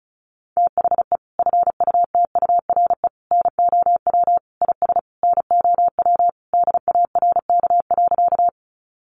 Morse code, heard as a keyed tone, about 34 wpm, 700 Hz. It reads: THE FUTURE NOW IS NOW DARK.
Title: Sentences from Top 700 Words 35wpm